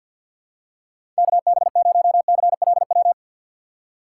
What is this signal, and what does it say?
Morse code, heard as a keyed tone, about 53 wpm, 700 Hz. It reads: KB0CLW
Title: USA Call Signs - 6 Characters - 4x Character Spacing - 50wpm